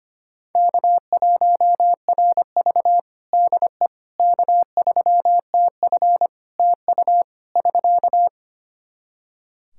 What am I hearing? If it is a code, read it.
Morse code, heard as a keyed tone, about 25 wpm, 700 Hz. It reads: K1RV DE K3TF TU <SK>
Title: Practice QSO - Exchange 4 - Ending - 25wpm